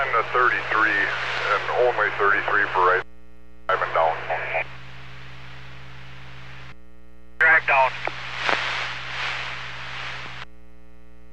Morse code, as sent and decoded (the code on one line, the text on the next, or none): none